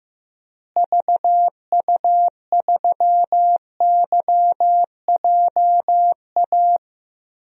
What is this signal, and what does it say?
Morse code, heard as a keyed tone, about 15 wpm, 700 Hz. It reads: VU3YJA
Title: DXCC-100 Call Signs - 6 Characters - Repeated 2x - 15wpm